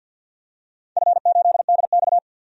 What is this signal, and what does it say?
Morse code, heard as a keyed tone, about 50 wpm, 700 Hz. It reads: F8DX
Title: DXCC-100 Call Signs - 4 Characters - 50wpm